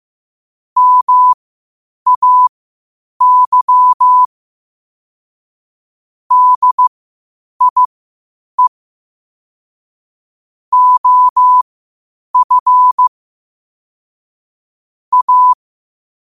Morse code, heard as a keyed tone, about 15 wpm, spaced as for 8 wpm, 1000 Hz. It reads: MAY DIE OF A
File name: qotdfiles / 8wpm